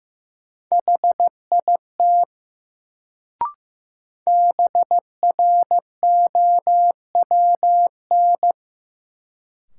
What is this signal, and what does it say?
Morse code, heard as a keyed tone, about 16 wpm, 700 Hz. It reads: HIT BROWN
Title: Sentences from Top 1200 Words - New Word Drill - 15wpm